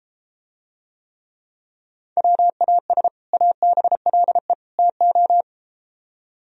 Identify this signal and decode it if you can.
Morse code, heard as a keyed tone, about 32 wpm, 700 Hz. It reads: WAS ABLE TO